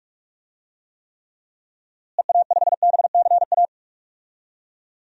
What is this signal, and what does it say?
Morse code, heard as a keyed tone, about 47 wpm, 700 Hz. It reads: EA5BCA
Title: DXCC-100 Call Signs - 6 Characters - Repeated 2x - 45wpm